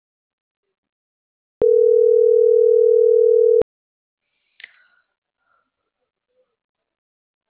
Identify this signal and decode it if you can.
Ringback tone (North America), 440 + 480 Hz.